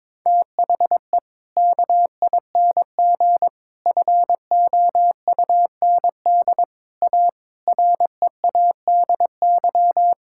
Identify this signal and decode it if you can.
Morse code, heard as a keyed tone, about 21 wpm, 700 Hz. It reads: THE KING FOUND A READY
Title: Sentences from Top 1000 Words - Review Part 5 - 22wpm